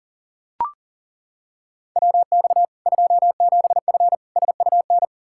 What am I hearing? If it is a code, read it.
Morse code, heard as a keyed tone, about 40 wpm, 700 Hz. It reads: WX 27F SUN
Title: Practice QSO - Exchange 2 - Info - 40wpm